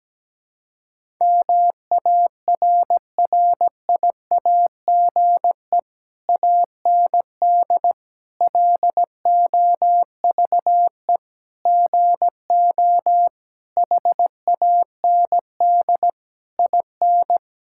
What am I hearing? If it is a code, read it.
Morse code, heard as a keyed tone, about 17 wpm, 700 Hz. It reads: MARRIAGE AND LOVE GO HAND IN
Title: Sentences from Top 1000 Words - Review Part 5 - 17wpm